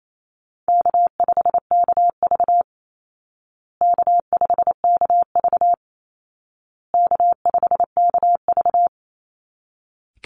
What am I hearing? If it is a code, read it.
Morse code, heard as a keyed tone, about 28 wpm, 700 Hz. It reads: K5KV K5KV K5KV
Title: Call Signs Repeated 2x - 28wpm